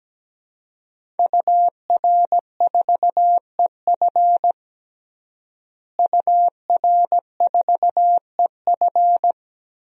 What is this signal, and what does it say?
Morse code, heard as a keyed tone, about 17 wpm, 700 Hz. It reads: UR4EF UR4EF
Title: DXCC-100 Call Signs - 5 Characters - Repeated 2x - 17wpm